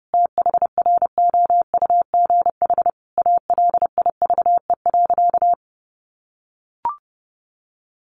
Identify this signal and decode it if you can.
Morse code, heard as a keyed tone, about 30 wpm, 700 Hz. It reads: THROUGH ALIVE.
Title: A Sound Of Thunder 30wpm